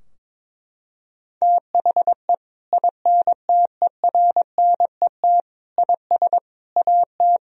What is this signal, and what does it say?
Morse code, heard as a keyed tone, about 22 wpm, 700 Hz. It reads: THE INTERNET IS AT
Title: Sentences from Top 1000 Words - Review Part 4 - 22wpm